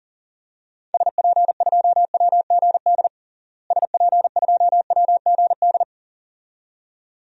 Morse code, heard as a keyed tone, about 40 wpm, 700 Hz. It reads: SP2WGD SP2WGD
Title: DXCC-100 Call Signs - 6 Characters - Repeated 2x - 40wpm